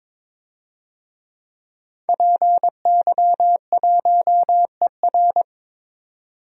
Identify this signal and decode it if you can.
Morse code, heard as a keyed tone, about 22 wpm, 700 Hz. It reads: PY1ER
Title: DXCC-100 Call Signs - 5 Characters - 22wpm